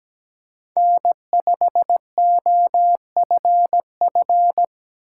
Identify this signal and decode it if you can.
Morse code, heard as a keyed tone, about 17 wpm, 700 Hz. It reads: N5OFF
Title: USA Call Signs - 5 Characters - 6x Character Spacing - 17wpm